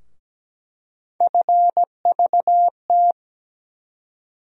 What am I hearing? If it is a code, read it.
Morse code, heard as a keyed tone, about 17 wpm, 700 Hz. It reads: FVT
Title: USA Call Sign Postfixes - 3 Letters 17wpm